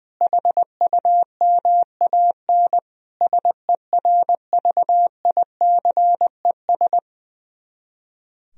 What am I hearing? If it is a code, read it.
Morse code, heard as a keyed tone, about 20 wpm, 700 Hz. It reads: HUMAN SERVICES
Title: Speed-Racing - Sets of 5 Words 20wpm